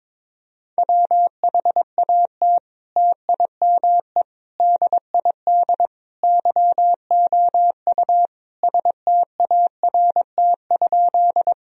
Morse code, heard as a keyed tone, about 22 wpm, 700 Hz. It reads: WHAT TIME DID YOU START?
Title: Sentences from Top 1000 Words - Review Part 1 - 22wpm